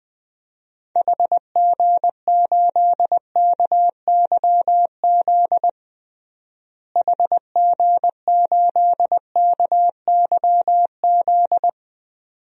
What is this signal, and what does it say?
Morse code, heard as a keyed tone, about 20 wpm, 700 Hz. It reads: HG8KYZ HG8KYZ
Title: DXCC-100 Call Signs - 6 Characters - Repeated 2x - 20wpm